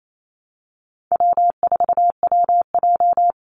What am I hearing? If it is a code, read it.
Morse code, heard as a keyed tone, about 28 wpm, 700 Hz. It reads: W4WJ
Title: Easy Call Signs - 28wpm